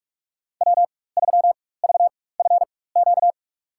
Morse code, heard as a keyed tone, about 46 wpm, spaced as for 21 wpm, 700 Hz. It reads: W3VFQ
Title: USA Call Signs - 5 Characters - 4x Character Spacing - 45wpm